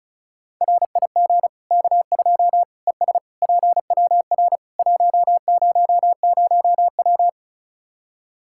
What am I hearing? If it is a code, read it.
Morse code, heard as a keyed tone, about 36 wpm, 700 Hz. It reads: RIG K2 ES PWR 100W
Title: Practice QSO - Exchange 2 - Info - 35wpm